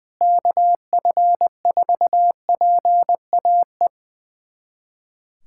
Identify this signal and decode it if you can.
Morse code, heard as a keyed tone, about 21 wpm, 700 Hz. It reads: KF4PAE
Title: USA Callsign DE Callsign 20wpm